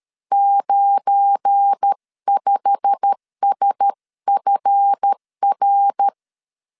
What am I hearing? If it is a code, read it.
Morse code, heard as a keyed tone, about 13 wpm, 800 Hz. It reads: 95SFR
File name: CW / DET / 1 9 1500C S B 12wpm 3819